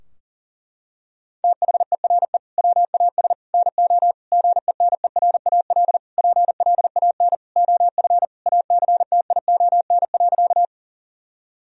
Morse code, heard as a keyed tone, about 40 wpm, 700 Hz. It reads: THERE WAS NO GENERAL PLAN OF ACTION.
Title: Sentences from Top 1000 Words - Review Part 4 - 40wpm